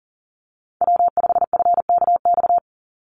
Morse code, heard as a keyed tone, about 39 wpm, 700 Hz. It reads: W5FKX
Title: Call Signs Repeated 3x - 40wpm